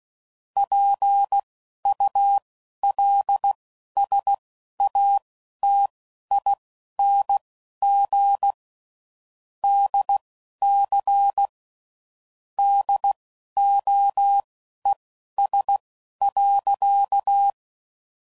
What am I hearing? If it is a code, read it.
Morse code, heard as a keyed tone, about 16 wpm, spaced as for 11 wpm, 800 Hz. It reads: PULSATING DC DOES.
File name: CW Code Groups @10WPM / Technical Subjects / SwPwrSup